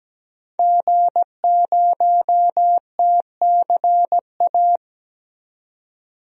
Morse code, heard as a keyed tone, about 17 wpm, 700 Hz. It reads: G0TCA
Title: DXCC-100 Call Signs - 5 Characters - 17wpm